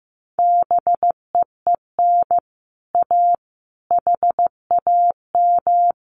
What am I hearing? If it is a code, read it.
Morse code, heard as a keyed tone, about 15 wpm, 700 Hz. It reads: BEEN A HAM
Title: Common QSO Phrases Encore 15wpm